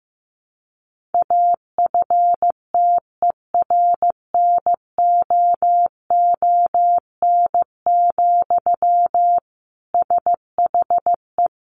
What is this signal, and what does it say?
Morse code, heard as a keyed tone, about 15 wpm, 700 Hz. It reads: AFTERNOON, SHE
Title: Sentences from Top 1500 Words 15wpm